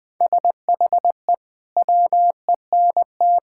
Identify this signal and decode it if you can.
Morse code, heard as a keyed tone, about 19 wpm, 700 Hz. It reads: SHE WENT
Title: Sentences with 5-Word Phrases - Spoken Beginning - 20wpm